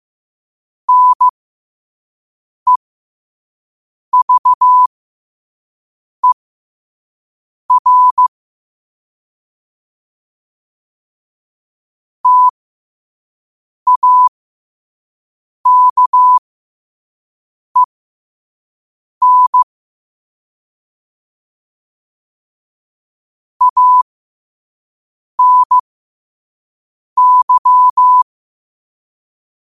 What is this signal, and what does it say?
Morse code, heard as a keyed tone, about 15 wpm, spaced as for 5 wpm, 1000 Hz. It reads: NEVER TAKEN ANY
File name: qotdfiles / 5wpm